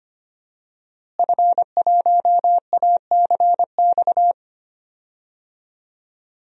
Morse code, heard as a keyed tone, about 25 wpm, 700 Hz. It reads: F1ACX
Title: DXCC-100 Call Signs - 5 Characters - Repeated 3x - 25wpm